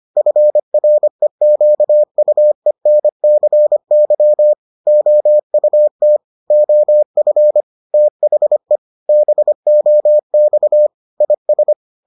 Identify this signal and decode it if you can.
Morse code, heard as a keyed tone, about 25 wpm, 600 Hz. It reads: FREQUENCY OUT OF THE BOX IS